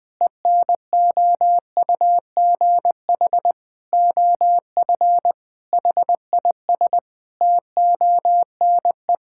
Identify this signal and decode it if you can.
Morse code, heard as a keyed tone, about 20 wpm, 700 Hz. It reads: ENOUGH OF HIS TONE
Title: Sentences from Top 1600 Words 20wpm